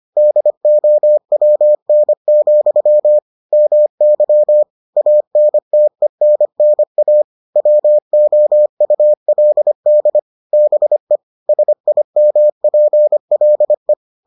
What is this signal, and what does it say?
Morse code, heard as a keyed tone, about 25 wpm, 600 Hz. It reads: DOWN, MY ANTENNA WOULD BE SIMPLE